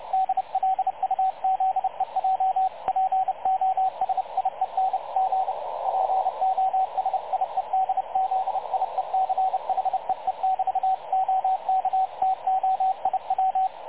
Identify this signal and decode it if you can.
Morse code, heard as a keyed tone, about 30 wpm, 760 Hz, with radio noise. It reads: DLUZEJ POSIEN G POSIEDZIEC HEE <BT> OK TO IW
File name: dlazaawansowanych / nagraniazpasma / websdr recording 3567.0kHz